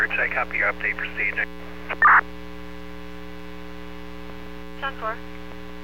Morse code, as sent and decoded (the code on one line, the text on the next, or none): none